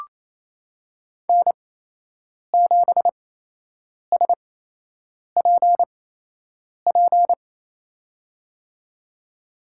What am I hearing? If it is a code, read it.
Morse code, heard as a keyed tone, about 28 wpm, spaced as for 8 wpm, 700 Hz. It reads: N7SPP
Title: USA Call Signs - 8x Character Spacing - 28wpm